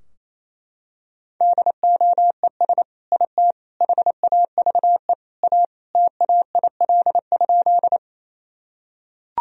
Morse code, heard as a keyed tone, about 28 wpm, 700 Hz. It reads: DOES IT HAVE A TAIL?
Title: Sentences from Top 1000 Words - Review Part 3 - 28wpm